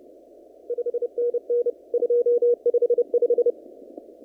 Morse code, heard as a keyed tone, about 28 wpm, 480 Hz. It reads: 5NN 255
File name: CW / kt5j wpx cw-op-k5pi